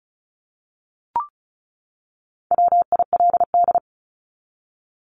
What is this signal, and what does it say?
Morse code, heard as a keyed tone, about 34 wpm, 700 Hz. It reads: WILD
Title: Top 1501-1750 Words 35wpm